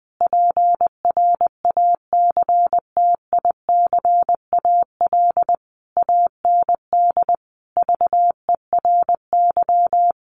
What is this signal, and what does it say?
Morse code, heard as a keyed tone, about 20 wpm, 700 Hz. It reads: PRACTICAL AND VERY
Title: Sentences from Top 1400 Words 20wpm